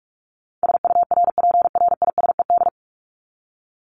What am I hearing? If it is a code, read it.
Morse code, heard as a keyed tone, about 44 wpm, 700 Hz. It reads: SURPRISED
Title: Top 1501-1750 Words 45wpm